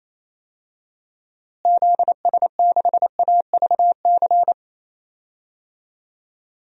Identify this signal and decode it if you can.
Morse code, heard as a keyed tone, about 28 wpm, 700 Hz. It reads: ZS6AVC
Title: DXCC-100 Call Signs - 6 Characters - Repeated 3x - 28wpm